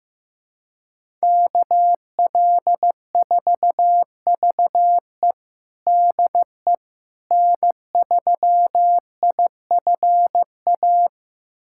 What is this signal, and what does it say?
Morse code, heard as a keyed tone, about 16 wpm, 700 Hz. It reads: KL4VE DE N3IFA
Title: USA Callsign DE Callsign - 5 Characters - 15wpm